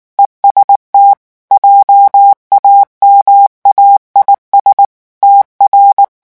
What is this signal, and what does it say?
Morse code, heard as a keyed tone, about 19 wpm, 800 Hz. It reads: EST JAMAIS TR
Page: REVISION 8 - 8 mots minute